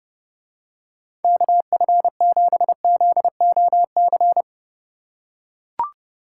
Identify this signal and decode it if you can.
Morse code, heard as a keyed tone, about 30 wpm, 700 Hz. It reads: KF7ZOC